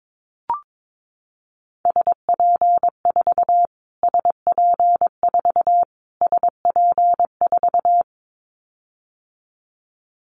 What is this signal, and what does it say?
Morse code, heard as a keyed tone, about 22 wpm, 700 Hz. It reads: SP4 SP4 SP4
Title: Callsign Prefixes for Top 25 DXCC - Repeated 3x - 22wpm